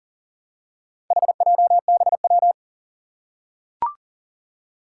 Morse code, heard as a keyed tone, about 44 wpm, 700 Hz. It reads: HJBW
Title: Four Challenging Letter-Number Combinations 40wpm